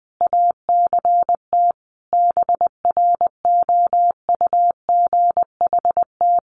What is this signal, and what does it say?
Morse code, heard as a keyed tone, about 20 wpm, 700 Hz. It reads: ACT BROUGHT
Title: Sentences from Top 1500 Words 20wpm